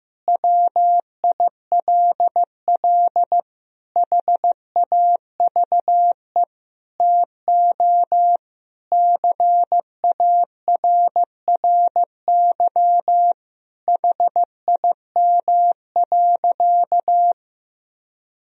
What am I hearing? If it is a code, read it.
Morse code, heard as a keyed tone, about 15 wpm, 700 Hz. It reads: WILL HAVE TO CARRY HIM.
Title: Sentences from Top 1000 Words - Review Part 1 - 15wpm